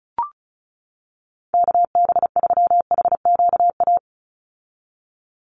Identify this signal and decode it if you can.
Morse code, heard as a keyed tone, about 34 wpm, 700 Hz. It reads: KB3HQA
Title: USA Call Signs 35wpm